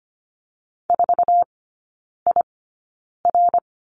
Morse code, heard as a keyed tone, about 25 wpm, spaced as for 9 wpm, 700 Hz. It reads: 4IR
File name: course / 25:10wpm / 093 Lesson 16e - Letter c - Three Characters 25wpm